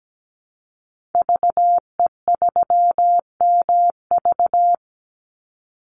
Morse code, heard as a keyed tone, about 17 wpm, 700 Hz. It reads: VE3MV
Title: Call Signs Repeated 3x - 17wpm